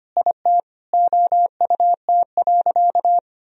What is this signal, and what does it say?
Morse code, heard as a keyed tone, about 25 wpm, 700 Hz. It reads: IT OUT.
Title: Sentences from Top 1000 Words - Review Part 3 - 25wpm